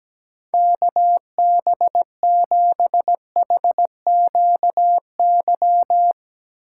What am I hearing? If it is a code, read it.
Morse code, heard as a keyed tone, about 17 wpm, 700 Hz. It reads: KB7HQY
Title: USA Call Signs - 6 Characters - 3x Character Spacing - 17wpm